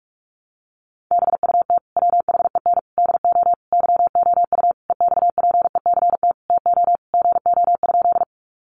Morse code, heard as a keyed tone, about 44 wpm, 700 Hz. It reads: BUT WHEN DO YOU EXPECT TO GO?
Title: Sentences from Top 1100 Words 45wpm